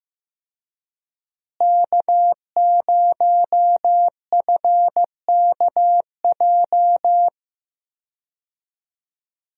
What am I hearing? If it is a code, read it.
Morse code, heard as a keyed tone, about 15 wpm, 700 Hz. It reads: K0FKJ
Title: USA Call Signs - 5 Characters - Repeated 2x - 15wpm